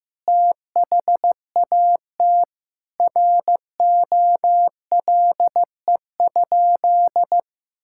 Morse code, heard as a keyed tone, about 15 wpm, 700 Hz. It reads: THAT ROLE?
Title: Sentences from Top 1000 Words - Review Part 6 - 15wpm